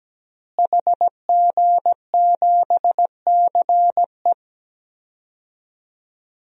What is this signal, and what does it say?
Morse code, heard as a keyed tone, about 17 wpm, 700 Hz. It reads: HG7CE
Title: DXCC-100 Call Signs - 5 Characters - Repeated 3x - 17wpm